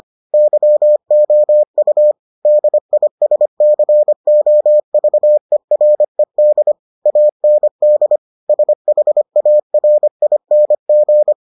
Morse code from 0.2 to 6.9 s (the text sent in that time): YOU DISCOVERED